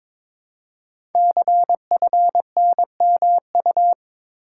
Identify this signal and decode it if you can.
Morse code, heard as a keyed tone, about 22 wpm, 700 Hz. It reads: CFNMU
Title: Code Groups - 5 Letters - 22wpm